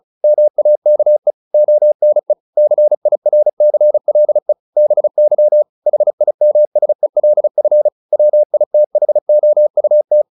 Morse code, heard as a keyed tone, about 34 wpm, 600 Hz. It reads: MAKE ONE CIRCLE BY HIMSELF WITHOUT
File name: the old man and the sea 2 35wpm